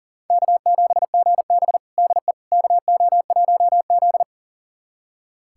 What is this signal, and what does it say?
Morse code, heard as a keyed tone, about 40 wpm, 700 Hz. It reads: K7GB DE KO1Z
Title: USA Callsign DE Callsign - 4 Characters - 40wpm